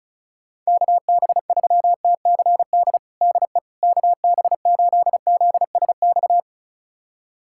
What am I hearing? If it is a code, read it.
Morse code, heard as a keyed tone, about 36 wpm, 690 Hz. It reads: KB3TCD DE KB8ZSX
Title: USA Callsign DE Callsign - 6 Characters - 35wpm